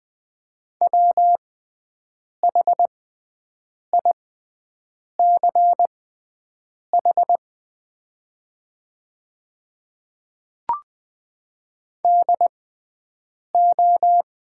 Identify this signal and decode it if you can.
Morse code, heard as a keyed tone, about 20 wpm, spaced as for 7 wpm, 700 Hz. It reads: WHICH DO
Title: 6x Character Spacing - Top 100 Words - 20wpm